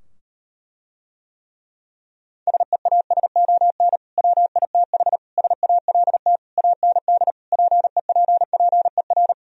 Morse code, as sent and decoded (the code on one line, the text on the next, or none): ... .- ... --- -. / .-- .. - .... / ... .- .-.. - / .- -. -.. / .--. .--. .--. .-.
SEASON WITH SALT AND PEPPER